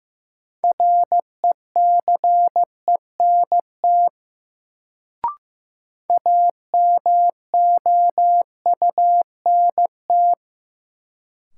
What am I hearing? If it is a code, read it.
Morse code, heard as a keyed tone, about 15 wpm, 700 Hz. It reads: RECENT AMOUNT
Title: Sets of 3 Words - New Word Drill - 15wpm